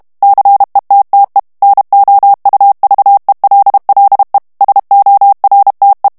Morse code, heard as a keyed tone, about 32 wpm, 800 Hz. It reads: CETTE NOUVELLE SORTE
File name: FRANCAIS TEXTE CLAIR 2 VITESSE 30 MOTS MINUTES